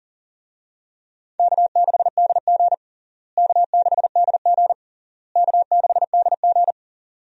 Morse code, heard as a keyed tone, about 37 wpm, 700 Hz. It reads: K6DG K6DG K6DG
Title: USA Call Signs - 4 Characters - Repeated 3x - 40wpm